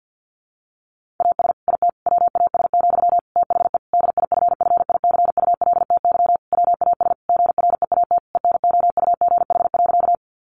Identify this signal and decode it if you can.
Morse code, heard as a keyed tone, about 50 wpm, 700 Hz. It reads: AS IT WAS, THE DIFFICULTY WAS GREAT ENOUGH.
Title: Sentences from Top 1500 Words 50wpm